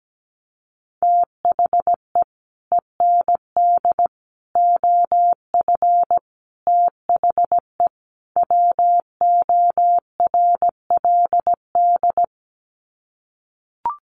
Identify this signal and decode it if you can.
Morse code, heard as a keyed tone, about 17 wpm, 700 Hz. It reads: THE END OF THE WORLD
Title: Sets of 5 Words 17wpm